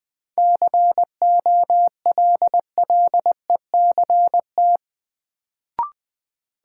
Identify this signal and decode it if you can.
Morse code, heard as a keyed tone, about 20 wpm, 700 Hz. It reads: COLLECT
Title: Sentences from Top 1600 Words - New Word Drill Alt - 20wpm